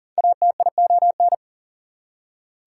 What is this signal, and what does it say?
Morse code, heard as a keyed tone, about 40 wpm, 700 Hz. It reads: ATION
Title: Prefixes and Suffixes 40wpm